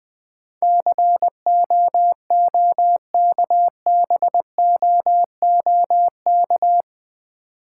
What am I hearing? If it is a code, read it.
Morse code, heard as a keyed tone, about 21 wpm, 700 Hz. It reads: COOKBOOK
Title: Difficult Words with Hints - 20wpm